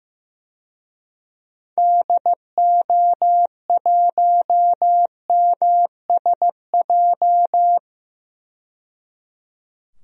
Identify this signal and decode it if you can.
Morse code, heard as a keyed tone, about 15 wpm, 700 Hz. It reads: DO1MSJ
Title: DXCC-100 Call Signs - 6 Characters - 15wpm